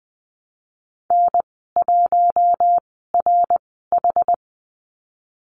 Morse code, heard as a keyed tone, about 20 wpm, spaced as for 15 wpm, 700 Hz. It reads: N1RH
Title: 069 Lesson 12g - Letter d - Call Signs 20wpm